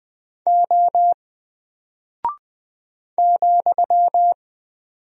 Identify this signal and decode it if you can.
Morse code, heard as a keyed tone, about 20 wpm, 700 Hz. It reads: O ,
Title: Single Character (Letter:Number:Punctuation:Prosign) 20wpm